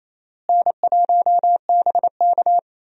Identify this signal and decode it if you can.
Morse code, heard as a keyed tone, about 28 wpm, 700 Hz. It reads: N1BK
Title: USA Call Signs - 4 Characters - Repeated 2x - 28wpm